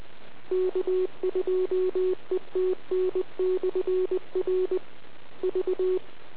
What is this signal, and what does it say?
Morse code, heard as a keyed tone, about 20 wpm, 370 Hz. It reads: K2ETN/R V